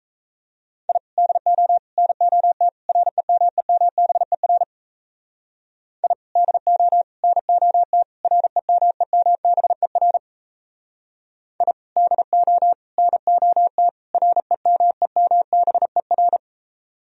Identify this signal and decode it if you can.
Morse code, heard as a keyed tone, about 37 wpm, 700 Hz. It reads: I DO NOT REMEMBER I DO NOT REMEMBER I DO NOT REMEMBER